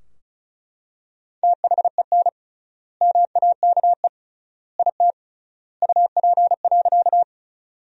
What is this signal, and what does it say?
Morse code, heard as a keyed tone, about 36 wpm, 700 Hz. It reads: THEN MAKE IT UP.
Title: 3x Word-Spacing - Sentences from Top 100 Words - 35wpm